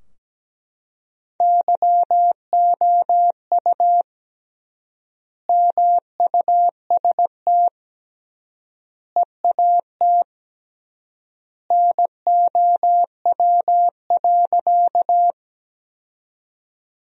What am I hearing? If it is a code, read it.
Morse code, heard as a keyed tone, about 17 wpm, 700 Hz. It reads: YOU MUST EAT NOW.
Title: 3x Word-Spacing - Sentences from Top 300 Words - 17wpm